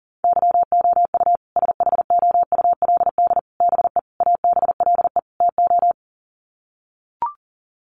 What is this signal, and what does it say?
Morse code, heard as a keyed tone, about 40 wpm, 700 Hz. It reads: YOU SHOULD BE ABLE TO
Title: Sets of 5 Words Encore 40wpm